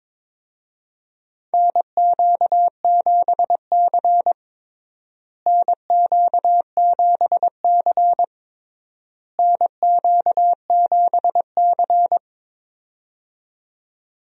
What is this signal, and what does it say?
Morse code, heard as a keyed tone, about 22 wpm, 700 Hz. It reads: NQ7C NQ7C NQ7C
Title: USA Call Signs - 4 Characters - Repeated 3x - 22wpm